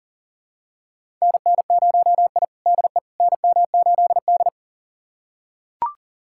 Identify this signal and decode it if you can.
Morse code, heard as a keyed tone, about 40 wpm, 700 Hz. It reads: NN0I DE NM8D
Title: USA Callsign DE Callsign - 4 Characters - 40wpm